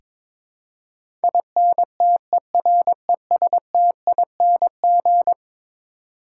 Morse code, heard as a keyed tone, about 21 wpm, 700 Hz. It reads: INTERESTING
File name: practice / 22wpm / Eleven Letter Words - 22wpm